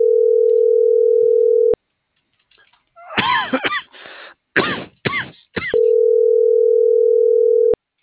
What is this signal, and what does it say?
Ringback tone (North America), 440 + 480 Hz.